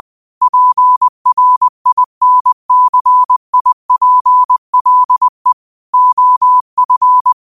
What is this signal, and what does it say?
Morse code, heard as a keyed tone, about 20 wpm, 1000 Hz. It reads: PRINCIPLE OF
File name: qotdfiles / 20wpm